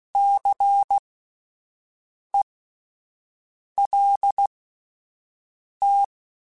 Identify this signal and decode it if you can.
Morse code, heard as a keyed tone, about 16 wpm, spaced as for 5 wpm, 800 Hz. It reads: CELT
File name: Alphabetical / Alphabetical mp3s / Cs @5